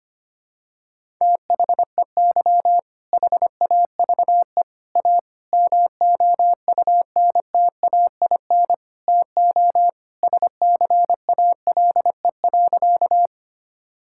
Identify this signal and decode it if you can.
Morse code, heard as a keyed tone, about 25 wpm, 700 Hz. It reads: THEY HAVE A MOUNTAIN TO SCALE.
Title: Sentences from Top 1000 Words - Review Part 5 - 25wpm